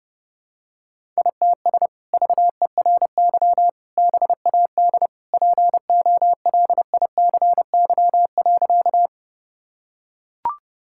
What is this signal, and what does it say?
Morse code, heard as a keyed tone, about 30 wpm, 700 Hz. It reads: ITS VERY BAD POLICY.
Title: Sentences from Top 1000 Words - Review Part 5 - 30wpm